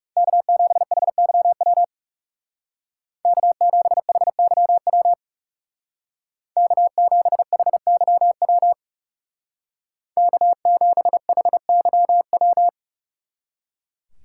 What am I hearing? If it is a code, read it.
Morse code, heard as a keyed tone, about 36 wpm, 700 Hz. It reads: K7HYW K7HYW K7HYW K7HYW
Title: Speed Racing - USA Call Signs 30wpm